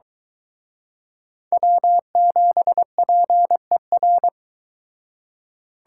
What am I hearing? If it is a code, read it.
Morse code, heard as a keyed tone, about 23 wpm, 700 Hz. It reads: W7PER